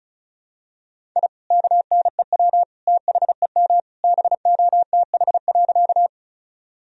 Morse code, heard as a keyed tone, about 36 wpm, 700 Hz. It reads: I KNEW THEM BOTH.
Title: Sentences from Top 1000 Words - Review Part 1 - 35wpm